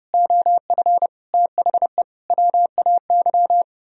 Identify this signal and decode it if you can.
Morse code, heard as a keyed tone, about 37 wpm, 700 Hz. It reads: OF THE WAY
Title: Speed-Racing - Sets of 4 Words 30wpm